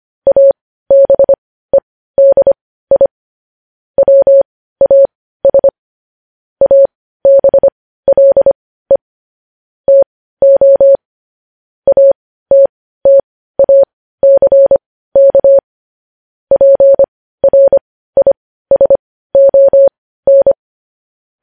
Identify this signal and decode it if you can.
Morse code, heard as a keyed tone, about 25 wpm, spaced as for 15 wpm, 550 Hz. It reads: ABEDI WAS ABLE TO ATTACK PRISON
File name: podcast / News headlines-1x-25x15